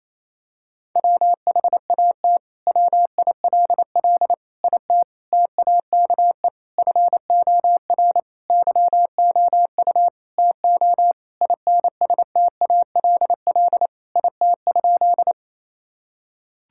Morse code, heard as a keyed tone, about 28 wpm, 700 Hz. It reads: WHAT WILL IT TAKE FOR YOU TO INSTALL IT?
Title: Sentences from Top 1600 Words 28wpm